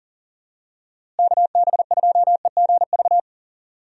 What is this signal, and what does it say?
Morse code, heard as a keyed tone, about 40 wpm, 700 Hz. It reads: KB2EGV
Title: USA Call Signs - 6 Characters - Repeated 3x - 40wpm